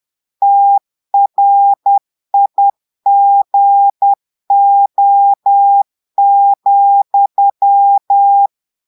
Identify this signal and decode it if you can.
Morse code, heard as a keyed tone, about 10 wpm, 800 Hz. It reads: TRIGO,
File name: Portugais - texte en clair 1 - vitesse - 10 mots minute